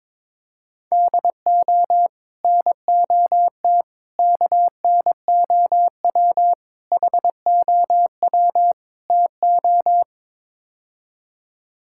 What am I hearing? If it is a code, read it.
Morse code, heard as a keyed tone, about 22 wpm, 700 Hz. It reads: DO NOT KNOW HOW TO
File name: practice / 22wpm / Build A Phrase - 5-Word Phrases - 22wpm